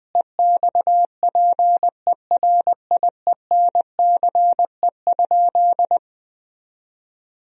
Morse code, heard as a keyed tone, about 21 wpm, 700 Hz. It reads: EXPERIENCE?
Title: Sentences from Top 1000 Words - Review Part 3 - 20wpm